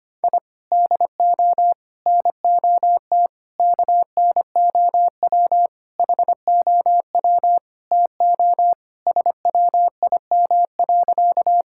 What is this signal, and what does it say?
Morse code, heard as a keyed tone, about 25 wpm, 700 Hz. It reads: I DO NOT KNOW HOW TO SWIM.
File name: practice / 25wpm / Sentences with 4-Word Phrases - 25wpm